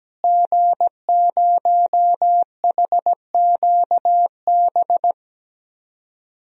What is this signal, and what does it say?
Morse code, heard as a keyed tone, about 17 wpm, 700 Hz. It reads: G0HQB
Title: DXCC-100 Call Signs - 5 Characters - Repeated 3x - 17wpm